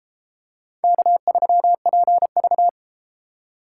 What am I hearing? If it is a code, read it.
Morse code, heard as a keyed tone, about 31 wpm, 700 Hz. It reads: K3PV